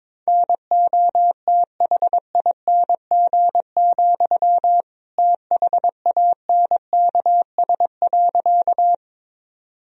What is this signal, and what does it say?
Morse code, heard as a keyed tone, about 22 wpm, 700 Hz. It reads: NOTHING, THANKS.
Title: Sentences from Top 1000 Words - Review Part 3 - 22wpm